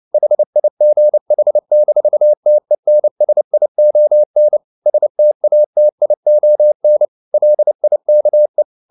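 Morse code, heard as a keyed tone, about 29 wpm, 600 Hz. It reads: HIGH-TENSION STATION LIKE
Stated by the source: Morse code at 29 wpm, sending Ken Follett's novel Never in English, from CW med Gnister.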